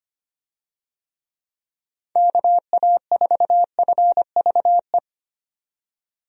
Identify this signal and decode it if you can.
Morse code, heard as a keyed tone, about 25 wpm, 700 Hz. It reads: KA4FVE
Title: USA Call Signs - 6 Characters - 25wpm